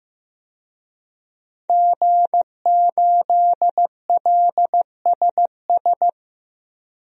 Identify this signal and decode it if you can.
Morse code, heard as a keyed tone, about 15 wpm, 700 Hz. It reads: G8LSS
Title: DXCC-100 Call Signs - 5 Characters - 15wpm